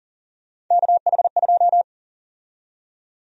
Morse code, heard as a keyed tone, about 40 wpm, 700 Hz. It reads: KH2
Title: USA Call Sign Prefixes - 40wpm